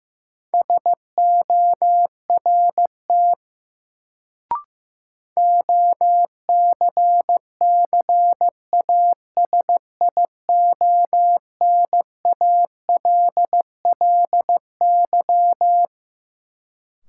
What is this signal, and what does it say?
Morse code, heard as a keyed tone, about 16 wpm, 700 Hz. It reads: SORT OCCASIONALLY
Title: Sentences from Top 1300 Words - New Word Drill Alt - 15wpm